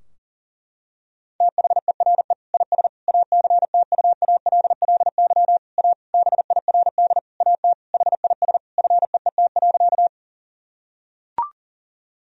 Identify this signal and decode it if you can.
Morse code, heard as a keyed tone, about 40 wpm, 700 Hz. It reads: THERE IS ACTUALLY A BIRD AT HIS FEET.
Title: Sentences from Top 1000 Words - Review Part 4 - 40wpm